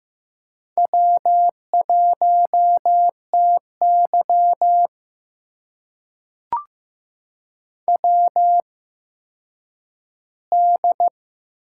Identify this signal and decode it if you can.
Morse code, heard as a keyed tone, about 15 wpm, spaced as for 9 wpm, 700 Hz. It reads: W1TY WD
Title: USA Call Signs - 4 Characters - 8x Character Spacing - 15wpm